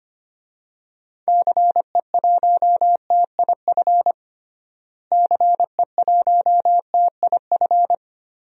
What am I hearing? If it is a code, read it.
Morse code, heard as a keyed tone, about 25 wpm, 700 Hz. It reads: CE1TIF CE1TIF
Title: DXCC-100 Call Signs - 6 Characters - Repeated 2x - 25wpm